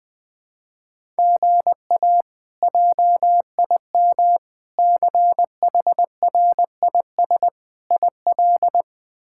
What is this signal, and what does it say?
Morse code, heard as a keyed tone, about 20 wpm, 700 Hz. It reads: GA JIM CHRIS IL
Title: K1USN SST QSO Practice - 20wpm